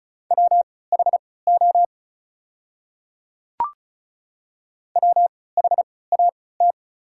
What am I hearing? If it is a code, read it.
Morse code, heard as a keyed tone, about 36 wpm, spaced as for 20 wpm, 690 Hz. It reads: WHO WHAT
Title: 3x Character Spacing - Top 100 Words - 35wpm